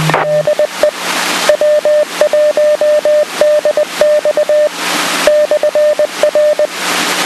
Morse code, heard as a keyed tone, about 20 wpm, 600 Hz, with noise. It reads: DE W1DX /R